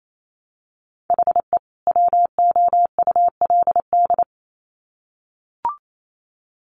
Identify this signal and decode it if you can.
Morse code, heard as a keyed tone, about 28 wpm, 700 Hz. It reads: HE WOULD
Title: Sets of 2 Words 28wpm